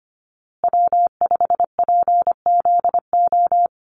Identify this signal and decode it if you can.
Morse code, heard as a keyed tone, about 25 wpm, 700 Hz. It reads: W5PZO
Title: USA Call Signs Repeated 2x 25wpm